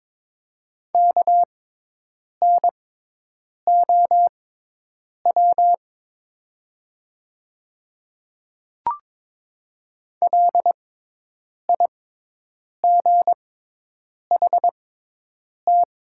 Morse code, heard as a keyed tone, about 22 wpm, spaced as for 8 wpm, 700 Hz. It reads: KNOW LIGHT
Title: 6x Character Spacing - Top 200 Words - 22wpm